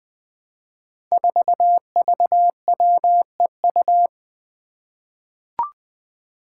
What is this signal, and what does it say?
Morse code, heard as a keyed tone, about 20 wpm, 700 Hz. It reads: 4VWEU
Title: Five Challenging Letter-Number Combinations 20wpm